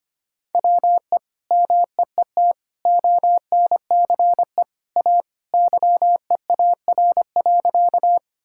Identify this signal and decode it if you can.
Morse code, heard as a keyed tone, about 25 wpm, 700 Hz. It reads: WE MEET ONCE A YEAR.
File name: practice / 25wpm / Sentences with 2-Word Phrases - 25wpm